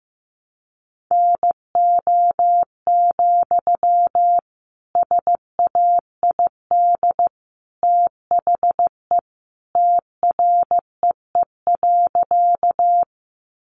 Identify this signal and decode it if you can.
Morse code, heard as a keyed tone, about 15 wpm, 700 Hz. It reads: NO, SAID THE TREE.
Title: Sentences from Top 300 Words 15wpm